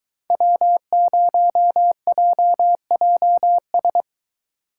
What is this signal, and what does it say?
Morse code, heard as a keyed tone, about 23 wpm, 700 Hz. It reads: W0JJS